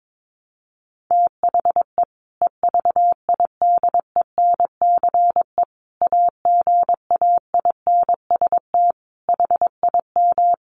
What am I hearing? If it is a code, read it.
Morse code, heard as a keyed tone, about 22 wpm, 700 Hz. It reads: THE EVIDENCE AGAINST HIM
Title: Sentences from Top 1300 Words 22wpm